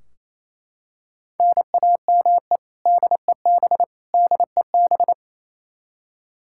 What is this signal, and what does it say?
Morse code, heard as a keyed tone, about 28 wpm, 700 Hz. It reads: NAME DEB DEB
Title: Practice QSO - Exchange 1 - Beginning - 28wpm